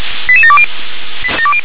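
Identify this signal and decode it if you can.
Five-tone selective call (ZVEI), heard as quick runs of five tones, tones of about 70 ms. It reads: 88720
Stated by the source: ZVEI five-tone call received by SDR.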